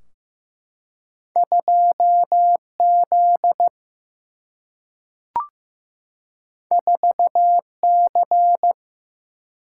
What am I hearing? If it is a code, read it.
Morse code, heard as a keyed tone, about 15 wpm, 700 Hz. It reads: 2Z 4C
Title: Single-Letter Single-Number 15wpm